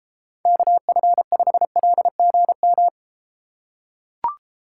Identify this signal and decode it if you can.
Morse code, heard as a keyed tone, about 32 wpm, 700 Hz. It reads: KF5LGM